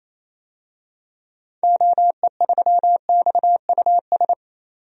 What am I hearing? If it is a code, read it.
Morse code, heard as a keyed tone, about 28 wpm, 700 Hz. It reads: OE3XUS
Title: DXCC-100 Call Signs - 6 Characters - 28wpm